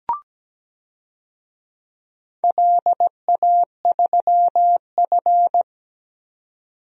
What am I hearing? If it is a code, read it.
Morse code, heard as a keyed tone, about 17 wpm, 700 Hz. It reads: LA3F
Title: DXCC-100 Call Signs - 4 Characters - 17wpm